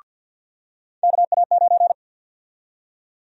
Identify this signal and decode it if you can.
Morse code, heard as a keyed tone, about 46 wpm, 700 Hz. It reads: KA9
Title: USA Call Sign Prefixes - 50wpm